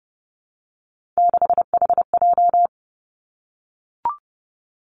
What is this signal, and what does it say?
Morse code, heard as a keyed tone, about 31 wpm, 700 Hz. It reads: 6HJ
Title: Three Challenging Letter-Number Combinations 30wpm